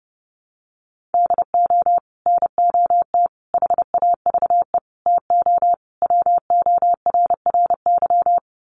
Morse code, heard as a keyed tone, about 30 wpm, 700 Hz. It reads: DO NOT HAVE TO WORRY
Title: Sets of 5 Words 30wpm